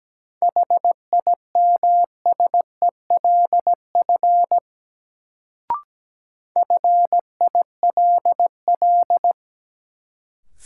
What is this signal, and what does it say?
Morse code, heard as a keyed tone, about 17 wpm, 700 Hz. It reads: HIMSELF FILL
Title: Sentences from Top 1500 Words - New Word Drill Alt - 17wpm